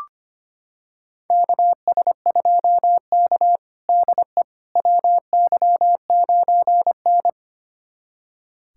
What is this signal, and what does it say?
Morse code, heard as a keyed tone, about 25 wpm, 700 Hz. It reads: KS2K DE WY9N
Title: USA Callsign DE Callsign - 4 Characters - 25wpm